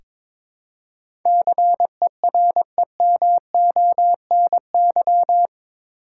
Morse code, heard as a keyed tone, about 22 wpm, 700 Hz. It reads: CEREMONY
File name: practice / 22wpm / Eight Letter Words - 22wpm